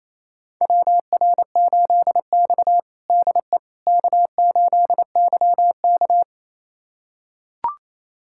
Morse code, heard as a keyed tone, about 28 wpm, 690 Hz. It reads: WR8X DE K8YK
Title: USA Callsign DE Callsign - 4 Characters - 28wpm